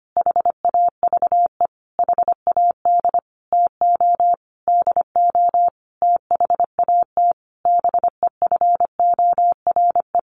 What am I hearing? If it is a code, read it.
Morse code, heard as a keyed tone, about 25 wpm, 700 Hz. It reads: HAVE HAD TO DO THAT BEFORE
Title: Sentences from Top 1300 Words 25wpm